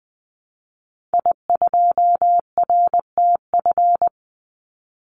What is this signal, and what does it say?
Morse code, heard as a keyed tone, about 20 wpm, 700 Hz. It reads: I2RTF
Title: Call Signs - 20wpm